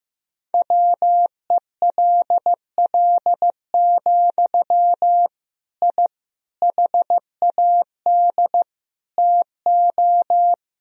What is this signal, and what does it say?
Morse code, heard as a keyed tone, about 15 wpm, 700 Hz. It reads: WELL, I HAD TO
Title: Sentences from Top 1000 Words - Review Part 3 - 15wpm